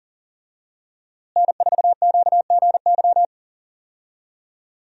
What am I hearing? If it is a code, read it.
Morse code, heard as a keyed tone, about 40 wpm, 700 Hz. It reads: N4QGY
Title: USA Call Signs - 5 Characters - Repeated 2x - 40wpm